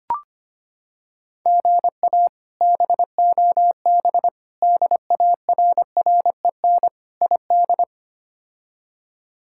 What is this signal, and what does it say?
Morse code, heard as a keyed tone, about 25 wpm, 700 Hz. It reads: GA BOB DARREN ID
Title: K1USN SST QSO Practice - 25wpm